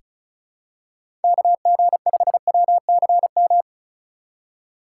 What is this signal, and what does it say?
Morse code, heard as a keyed tone, about 36 wpm, 700 Hz. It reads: KG5WCM
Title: USA Call Signs - 3x Character Spacing - 35wpm